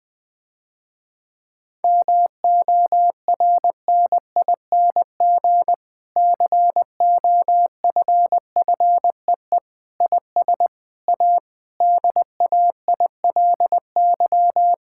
Morse code, heard as a keyed tone, about 21 wpm, 700 Hz. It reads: MORNING COFFEE IS A DAILY
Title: Difficult Sentences with Hints - 20wpm